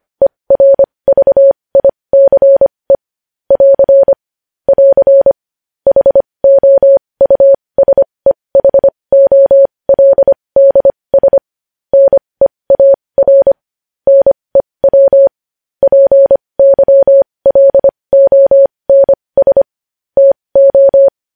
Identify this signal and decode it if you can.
Morse code, heard as a keyed tone, about 25 wpm, spaced as for 20 wpm, 550 Hz. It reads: ERVICE <AR> <AR> HOUSEHOLDS NEAR NEW PYLONS TO
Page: News Headlines in Morse Code at 20 WPM